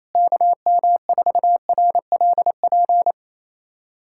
Morse code, heard as a keyed tone, about 28 wpm, 700 Hz. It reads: KM4RLP
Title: USA Callsign DE Callsign - 6 Characters - 28wpm